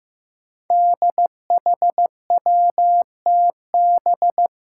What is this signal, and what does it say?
Morse code, heard as a keyed tone, about 15 wpm, 700 Hz. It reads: DHWTB
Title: Five Challenging Letter-Number Combinations 15wpm